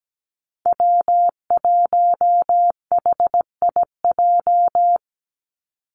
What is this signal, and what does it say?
Morse code, heard as a keyed tone, about 17 wpm, 700 Hz. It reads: W1HIJ
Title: Call Signs Repeated 3x - 17wpm